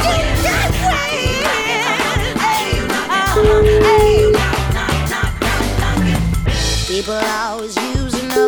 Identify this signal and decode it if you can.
Ringback tone (Europe, 425 Hz), with speech.